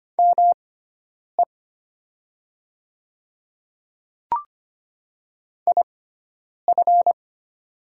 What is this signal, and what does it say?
Morse code, heard as a keyed tone, about 25 wpm, spaced as for 9 wpm, 700 Hz. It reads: ME IF
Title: 6x Character Spacing - Top 200 Words - 25wpm